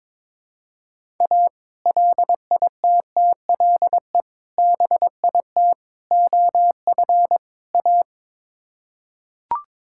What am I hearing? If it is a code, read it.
Morse code, heard as a keyed tone, about 21 wpm, 700 Hz. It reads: A LITTLE BIT OF A
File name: practice / 22wpm / Build A Phrase - 5-Word Phrases - 22wpm